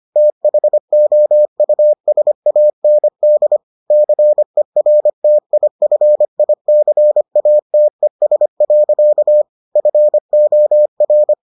Morse code, heard as a keyed tone, about 25 wpm, 600 Hz. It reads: THOUSAND CERTIFICATES. FOR